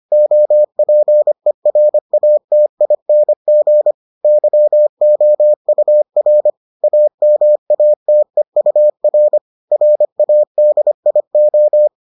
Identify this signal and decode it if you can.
Morse code, heard as a keyed tone, about 25 wpm, 600 Hz. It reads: OPERATING YOUR AMATEUR RADIO